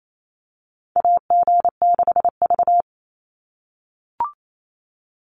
Morse code, heard as a keyed tone, about 28 wpm, 700 Hz. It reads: AG6V
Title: NA SOTA Chaser Callsigns 28wpm